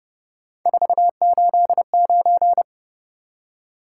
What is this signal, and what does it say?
Morse code, heard as a keyed tone, about 30 wpm, 700 Hz. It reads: 489
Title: Top 180 QSO Elements 30wpm